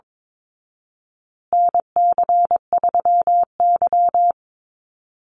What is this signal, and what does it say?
Morse code, heard as a keyed tone, about 22 wpm, 700 Hz. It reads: NC3Y
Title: Call Signs Repeated 2x - 22wpm